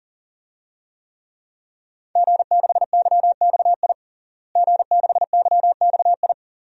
Morse code, heard as a keyed tone, about 40 wpm, 700 Hz. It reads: G6YXI G6YXI
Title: DXCC-100 Call Signs - 5 Characters - Repeated 3x - 40wpm